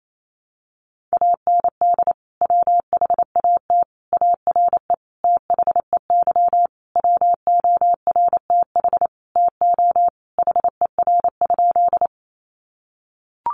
Morse code, heard as a keyed tone, about 28 wpm, 700 Hz. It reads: AND WHAT ARE THEY WORTH TO HER?
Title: Sentences from Top 800 Words 28wpm